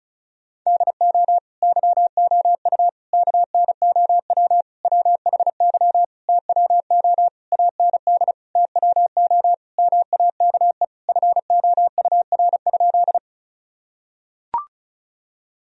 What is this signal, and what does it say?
Morse code, heard as a keyed tone, about 34 wpm, 700 Hz. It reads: DO YOU KNOW WHY TWO AND TWO MAKE FOUR?
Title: Sentences from Top 1000 Words - Review Part 1 - 35wpm